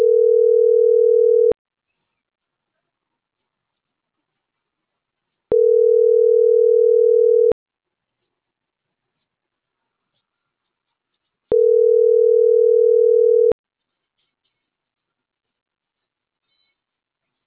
Ringback tone (North America), 440 + 480 Hz.